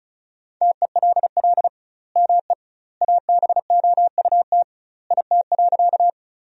Morse code, heard as a keyed tone, about 36 wpm, 700 Hz. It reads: TELL ME ABOUT IT.
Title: 2x Word-Spacing - Sentences from Top 200 Words - 35wpm